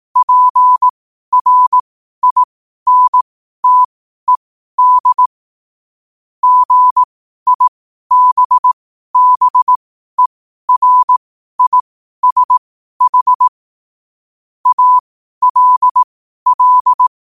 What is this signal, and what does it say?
Morse code, heard as a keyed tone, about 18 wpm, spaced as for 12 wpm, 1000 Hz. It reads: PRINTED GIBBERISH ALL
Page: New quotes every day in morse code at 12 Words per minute.